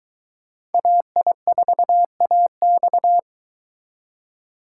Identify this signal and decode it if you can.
Morse code, heard as a keyed tone, about 22 wpm, 700 Hz. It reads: AI4AX